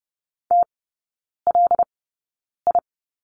Morse code, heard as a keyed tone, about 30 wpm, spaced as for 9 wpm, 700 Hz. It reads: TLI
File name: Lesson 15e - Number 5 - Three Characters 30wpm